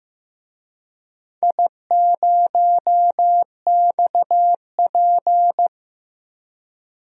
Morse code, heard as a keyed tone, about 15 wpm, 700 Hz. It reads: I0XP
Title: DXCC-100 Call Signs - 4 Characters - Repeated 3x - 15wpm